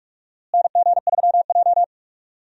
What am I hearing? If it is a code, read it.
Morse code, heard as a keyed tone, about 46 wpm, 700 Hz. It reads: NG3J
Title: USA Call Signs - 4 Characters - Repeated 2x - 45wpm